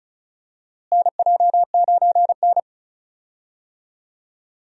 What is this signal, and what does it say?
Morse code, heard as a keyed tone, about 36 wpm, 700 Hz. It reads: NJ9N
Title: USA Call Signs - 4 Characters - Repeated 2x - 35wpm